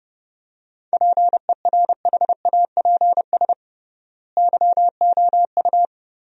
Morse code, heard as a keyed tone, about 29 wpm, 700 Hz. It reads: PERHAPS YOU
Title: 3x Word-Spacing - Sentences from Top 500 Words - 30wpm